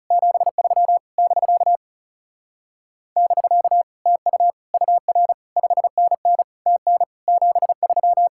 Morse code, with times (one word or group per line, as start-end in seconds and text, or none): 0.1-1.0 73
1.2-1.8 <BK>
3.2-3.8 <BK>
4.1-4.5 TU
4.7-5.3 UR
5.6-6.4 5NN
6.7-7.0 TN
7.3-8.3 73